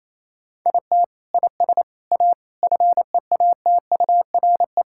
Morse code, heard as a keyed tone, about 28 wpm, 700 Hz. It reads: IT IS A FEATURE
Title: Sentences from Top 1000 Words - Review Part 5 - 28wpm